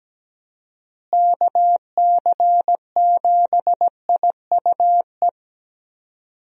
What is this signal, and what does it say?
Morse code, heard as a keyed tone, about 17 wpm, 700 Hz. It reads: KC7IUE
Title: USA Call Signs - 6 Characters - Repeated 3x - 17wpm